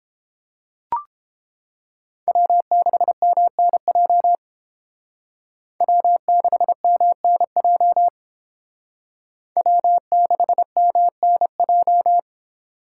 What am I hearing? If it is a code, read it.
Morse code, heard as a keyed tone, about 29 wpm, 700 Hz. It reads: W6MNJ W6MNJ W6MNJ